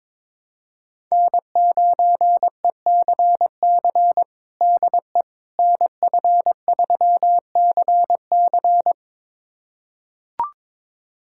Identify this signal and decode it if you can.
Morse code, heard as a keyed tone, about 21 wpm, 700 Hz. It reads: N9ECC DE NF3CC
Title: USA Callsign DE Callsign 22wpm